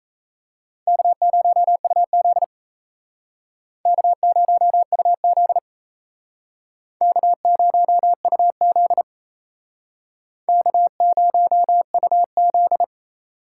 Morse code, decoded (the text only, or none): K0UZ K0UZ K0UZ K0UZ